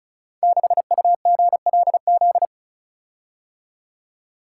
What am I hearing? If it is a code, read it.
Morse code, heard as a keyed tone, about 36 wpm, 700 Hz. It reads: 6UGLZ
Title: Code Groups - 5 Letter:Numbers - 35wpm